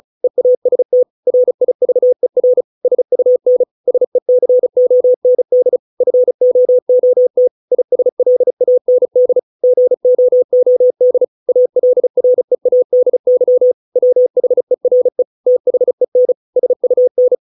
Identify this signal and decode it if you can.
Morse code, heard as a keyed tone, about 34 wpm, 500 Hz. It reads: EAST RIVER SUN SECOND FOOT ISLAND GOOD ALREADY WHERE THEN SUN